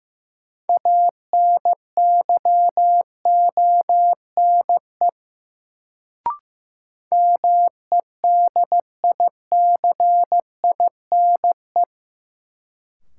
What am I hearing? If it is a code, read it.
Morse code, heard as a keyed tone, about 15 wpm, 700 Hz. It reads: ANYONE MEDICINE
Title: Sentences from Top 900 Words - New Word Drill - 15wpm